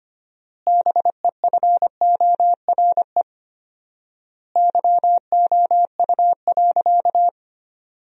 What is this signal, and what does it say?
Morse code, heard as a keyed tone, about 25 wpm, 700 Hz. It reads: BEFORE YOU.
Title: 4x Word-Spacing - Sentences from Top 200 Words - 25wpm